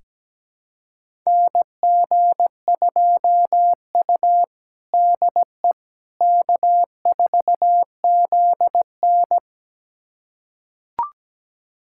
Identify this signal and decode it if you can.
Morse code, heard as a keyed tone, about 17 wpm, 700 Hz. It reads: NG2U DE K4ZN
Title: USA Callsign DE Callsign - 4 Characters - 17wpm